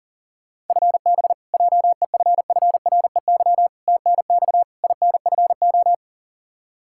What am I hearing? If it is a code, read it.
Morse code, heard as a keyed tone, about 38 wpm, 700 Hz. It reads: FB JEFFREY TNX INFO
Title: K1USN SST QSO Practice 40wpm